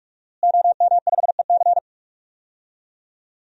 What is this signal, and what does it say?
Morse code, heard as a keyed tone, about 46 wpm, 700 Hz. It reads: OM5EC
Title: DXCC-100 Call Signs - 5 Characters - Repeated 3x - 45wpm